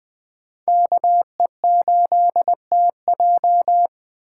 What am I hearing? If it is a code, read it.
Morse code, heard as a keyed tone, about 20 wpm, 700 Hz. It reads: KE8TJ
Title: USA Call Signs - 5 Characters - 2x Character Spacing - 20wpm